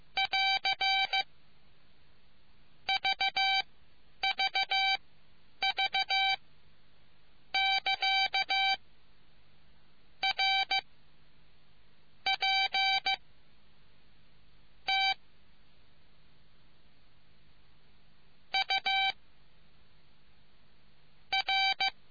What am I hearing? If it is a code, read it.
Morse code, heard as a keyed tone, about 15 wpm, spaced as for 9 wpm, 780 Hz. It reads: <AR> VVV <KA> R P T U R